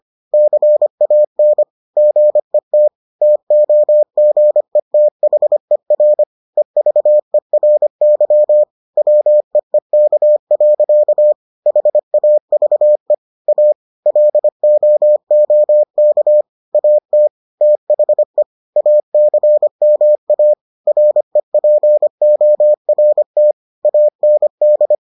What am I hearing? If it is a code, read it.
Morse code, heard as a keyed tone, about 25 wpm, 600 Hz. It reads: CAN GET TOGETHER EVERY WEEK. HAVE A LOOK AT THE ACMA REPORT AND